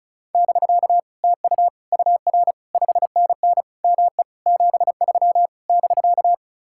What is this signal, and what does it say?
Morse code, heard as a keyed tone, about 47 wpm, 700 Hz. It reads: <BK> TU UR 5NN ME 73 <BK>